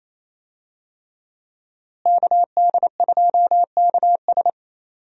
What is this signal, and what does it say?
Morse code, heard as a keyed tone, about 28 wpm, 700 Hz. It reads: KD2KS
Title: USA Call Signs - 5 Characters - Repeated 3x - 28wpm